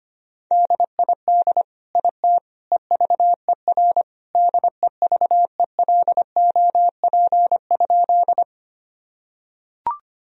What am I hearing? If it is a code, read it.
Morse code, heard as a keyed tone, about 25 wpm, 700 Hz. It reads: DID IT EVER DEVELOP?
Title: Sentences from Top 1000 Words - Review Part 2 - 25wpm